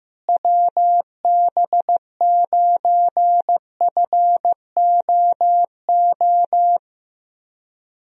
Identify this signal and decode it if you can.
Morse code, heard as a keyed tone, about 15 wpm, 700 Hz. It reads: WB9FOO
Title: USA Call Signs - 6 Characters - Repeated 3x - 15wpm